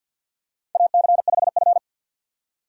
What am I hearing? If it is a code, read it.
Morse code, heard as a keyed tone, about 50 wpm, 700 Hz. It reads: AC5F
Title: USA Call Signs - 4 Characters - 6x Character Spacing - 50wpm